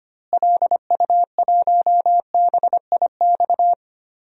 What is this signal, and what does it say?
Morse code, heard as a keyed tone, about 25 wpm, 700 Hz. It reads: LU1BIX
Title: DXCC-100 Call Signs - 6 Characters - Repeated 3x - 25wpm